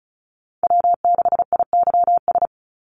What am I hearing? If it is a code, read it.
Morse code, heard as a keyed tone, about 34 wpm, 700 Hz. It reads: W6IYS
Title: Call Signs - 35wpm